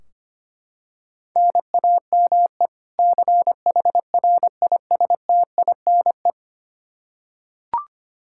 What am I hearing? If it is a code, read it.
Morse code, heard as a keyed tone, about 25 wpm, 700 Hz. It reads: NAME CHRISTINE
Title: Practice QSO - Exchange 1 - Beginning - 25wpm